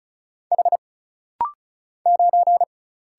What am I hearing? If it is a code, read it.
Morse code, heard as a keyed tone, about 35 wpm, 700 Hz. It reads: H 9
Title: Single Letter-Number - Rapid-Fire 35wpm